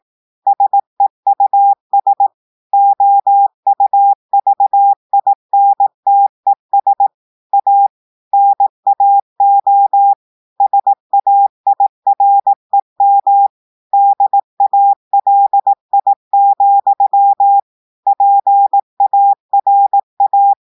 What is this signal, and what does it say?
Morse code, heard as a keyed tone, about 18 wpm, 800 Hz. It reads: SEUS OUVINTES A NAO SAIREM DALI, PARA
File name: audios / portugais / portugais texte en clair 2 vitesse 18 mots minutes